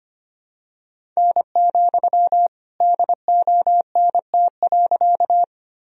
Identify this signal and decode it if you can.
Morse code, heard as a keyed tone, about 25 wpm, 700 Hz. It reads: N, DONT.
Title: Sentences from Top 300 Words and Abbreviations 25wpm